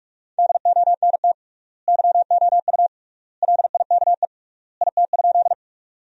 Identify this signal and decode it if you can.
Morse code, heard as a keyed tone, about 44 wpm, 700 Hz. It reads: DONT YOU LIKE IT?
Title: 3x Word-Spacing - Sentences from Top 300 Words - 45wpm